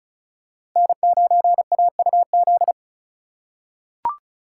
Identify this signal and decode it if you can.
Morse code, heard as a keyed tone, about 36 wpm, 700 Hz. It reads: N9AUZ
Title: USA Call Signs - 4x Character Spacing - 35wpm